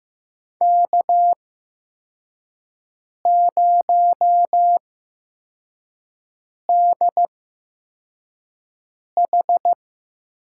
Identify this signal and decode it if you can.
Morse code, heard as a keyed tone, about 15 wpm, spaced as for 4 wpm, 700 Hz. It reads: K0DH
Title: USA Call Signs - 8x Character Spacing - 15wpm